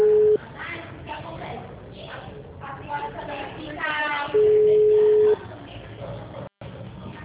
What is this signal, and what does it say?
Ringback tone (Europe, 425 Hz), with speech.